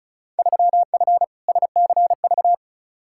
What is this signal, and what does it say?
Morse code, heard as a keyed tone, about 34 wpm, 700 Hz. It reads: 3F SCV
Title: Field Day Exchanges - 35wpm